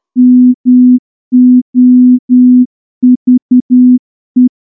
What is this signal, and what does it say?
Morse code, heard as a keyed tone, about 10 wpm, 250 Hz. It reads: MOVE
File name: morse-training-10WPM